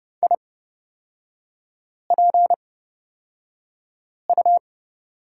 Morse code, heard as a keyed tone, about 31 wpm, spaced as for 5 wpm, 690 Hz. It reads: IPU
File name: course / 30:10wpm / 236 Lesson 38e - Letter q - Three Characters 30wpm